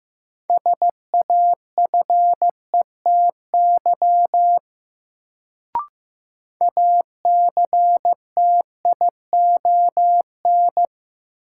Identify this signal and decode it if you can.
Morse code, heard as a keyed tone, about 15 wpm, 700 Hz. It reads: SAFETY ACTION
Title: Sentences from Top 700 Words - New Word Drill - 15wpm